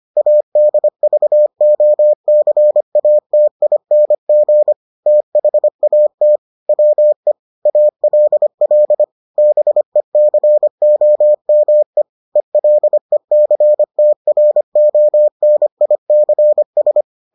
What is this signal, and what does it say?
Morse code, heard as a keyed tone, about 25 wpm, 600 Hz. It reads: ADVOCATING THAT WE ALL BECOME ELECTRONICS